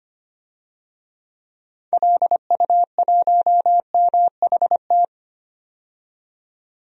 Morse code, heard as a keyed tone, about 25 wpm, 700 Hz. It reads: LU1MHT
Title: DXCC-100 Call Signs - 6 Characters - 25wpm